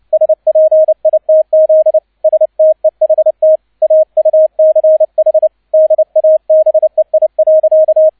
Morse code, heard as a keyed tone, about 29 wpm, 630 Hz, with radio noise. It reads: SPITZ STEHT AUCH DABEI.
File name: 002 Zweiter Streich 30wpm